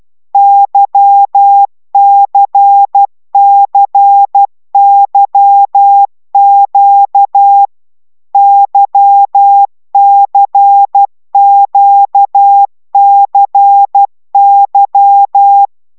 Morse code, heard as a keyed tone, about 12 wpm, 800 Hz. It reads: YCCYQ YCQCY
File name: taviro tanulas / cw 75 15 10